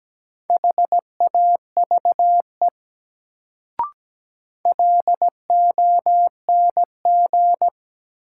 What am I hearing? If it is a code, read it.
Morse code, heard as a keyed tone, about 17 wpm, 700 Hz. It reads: HAVE LONG
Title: Sentences from Top 100 Words - New Word Drill - 17wpm